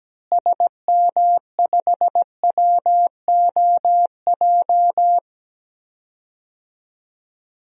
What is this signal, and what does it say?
Morse code, heard as a keyed tone, about 17 wpm, 700 Hz. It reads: SM5WOJ
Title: DXCC-100 Call Signs - 6 Characters - Repeated 3x - 17wpm